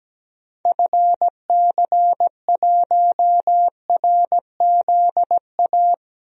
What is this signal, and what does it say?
Morse code, heard as a keyed tone, about 17 wpm, 700 Hz. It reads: FC1RZA
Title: DXCC-100 Call Signs - 6 Characters - Repeated 2x - 17wpm